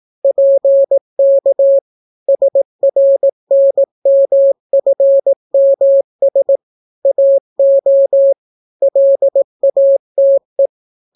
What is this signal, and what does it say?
Morse code, heard as a keyed tone, about 18 wpm, 550 Hz. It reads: PK SRNMFMS AO LATE